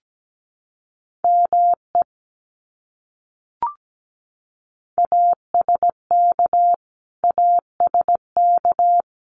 Morse code, heard as a keyed tone, about 17 wpm, 700 Hz. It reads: ME ASK ASK
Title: Top 200 Words Repeated 17wpm